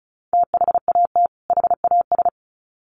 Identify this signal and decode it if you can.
Morse code, heard as a keyed tone, about 34 wpm, 700 Hz. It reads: THAT HAS
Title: Sets of 2 Words 35wpm